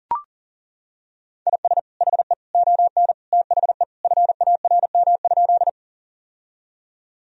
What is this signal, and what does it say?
Morse code, heard as a keyed tone, about 37 wpm, 700 Hz. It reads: IS HE ON THE FARM?
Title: Sentences from Top 1000 Words - Review Part 2 - 40wpm